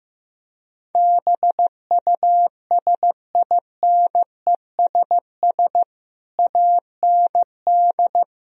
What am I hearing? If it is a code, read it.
Morse code, heard as a keyed tone, about 16 wpm, 700 Hz. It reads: BUSINESS AND
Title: Sentences from Top 1000 Words - Review Part 3 - 15wpm